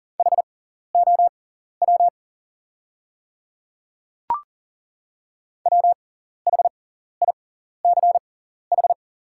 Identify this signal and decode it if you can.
Morse code, heard as a keyed tone, about 40 wpm, spaced as for 14 wpm, 690 Hz. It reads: HOW WHICH
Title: 6x Character Spacing - Top 100 Words - 40wpm